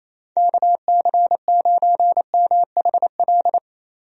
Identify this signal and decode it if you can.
Morse code, heard as a keyed tone, about 28 wpm, 700 Hz. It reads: KC9MHL
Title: USA Call Signs - 6 Characters - Repeated 2x - 28wpm